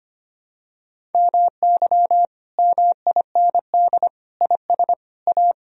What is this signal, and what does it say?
Morse code, heard as a keyed tone, about 25 wpm, 700 Hz. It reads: MY MIND IS A
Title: Sentences from Top 1000 Words - Review Part 1 - 25wpm